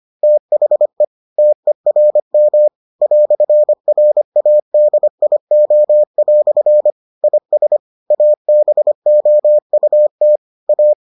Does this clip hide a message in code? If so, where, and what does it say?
Yes: Morse code from 0.2 to 10.9 s, heard as a keyed tone, about 25 wpm, 600 Hz. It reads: THE TERM "RADIO" IS ABOUT A